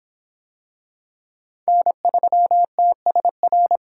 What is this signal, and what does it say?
Morse code, heard as a keyed tone, about 26 wpm, 700 Hz. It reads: N3TSR